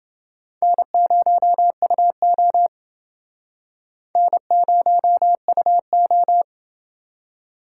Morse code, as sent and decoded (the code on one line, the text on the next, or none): -. ----- ..- --- / -. ----- ..- ---
N0UO N0UO